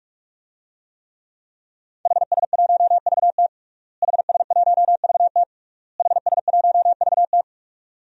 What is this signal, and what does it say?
Morse code, heard as a keyed tone, about 45 wpm, 700 Hz. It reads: HS1VT HS1VT HS1VT
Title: DXCC-100 Call Signs - 5 Characters - Repeated 3x - 45wpm